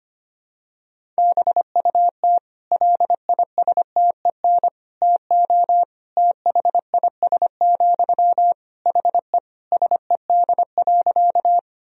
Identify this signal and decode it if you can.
Morse code, heard as a keyed tone, about 25 wpm, 700 Hz. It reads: BUT LISTEN TO THIS, HE SED.
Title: Sentences from Top 400 Words and Abbreviations 25wpm